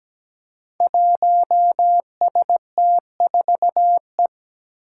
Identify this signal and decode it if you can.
Morse code, heard as a keyed tone, about 17 wpm, 700 Hz. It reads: 1ST4E
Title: Five Challenging Letter-Number Combinations 17wpm